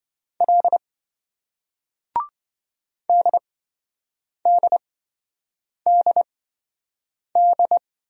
Morse code, heard as a keyed tone, about 27 wpm, spaced as for 7 wpm, 700 Hz. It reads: L DDDD